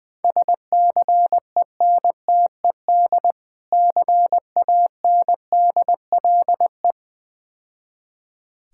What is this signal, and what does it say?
Morse code, heard as a keyed tone, about 20 wpm, 700 Hz. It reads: SCENTED CANDLE
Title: Difficult Words - Speed-Racing with Hints - 20wpm